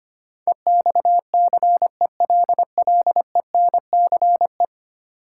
Morse code, heard as a keyed tone, about 25 wpm, 700 Hz. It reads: EXCELLENCE
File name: Ten Letter Words - 25wpm